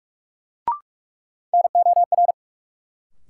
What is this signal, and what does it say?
Morse code, heard as a keyed tone, about 46 wpm, 700 Hz. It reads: NOR
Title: Sentences from Top 1500 Words - New Word Drill Alt - 45wpm